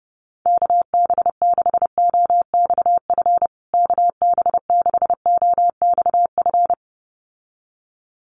Morse code, heard as a keyed tone, about 30 wpm, 700 Hz. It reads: KB6OXF KB6OXF
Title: USA Call Signs Repeated 2x 30wpm